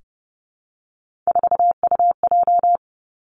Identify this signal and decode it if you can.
Morse code, heard as a keyed tone, about 30 wpm, 700 Hz. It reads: 4UJ
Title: Three Challenging Letter-Number Combinations 30wpm